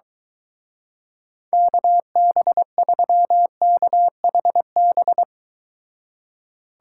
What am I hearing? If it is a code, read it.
Morse code, heard as a keyed tone, about 23 wpm, 700 Hz. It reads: KB3KHB